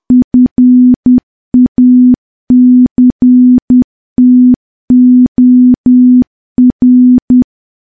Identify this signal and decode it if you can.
Morse code, heard as a keyed tone, about 10 wpm, 250 Hz. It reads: FACTOR
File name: morse / morse-training-10WPM